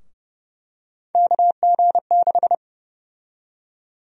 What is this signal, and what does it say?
Morse code, heard as a keyed tone, about 32 wpm, 700 Hz. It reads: KG6
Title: USA Call Sign Prefixes - Repeated 2x - 30wpm